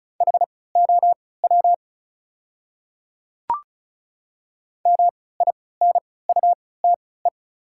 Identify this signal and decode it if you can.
Morse code, heard as a keyed tone, about 36 wpm, spaced as for 20 wpm, 690 Hz. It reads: HOW MINUTE
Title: 3x Character Spacing - Top 500 Words - 35wpm